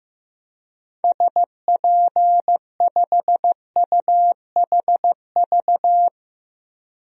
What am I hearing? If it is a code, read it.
Morse code, heard as a keyed tone, about 15 wpm, 700 Hz. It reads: SP5UHV
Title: DXCC-100 Call Signs - 6 Characters - 15wpm